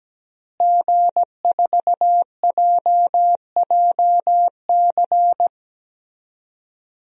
Morse code, heard as a keyed tone, about 17 wpm, 700 Hz. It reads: G4JJC
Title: DXCC-100 Call Signs - 5 Characters - Repeated 2x - 17wpm